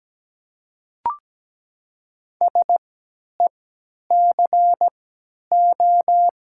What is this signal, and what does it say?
Morse code, heard as a keyed tone, about 17 wpm, spaced as for 10 wpm, 700 Hz. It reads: SECO
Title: 3x Character Spacing - Top 300 Words - 17wpm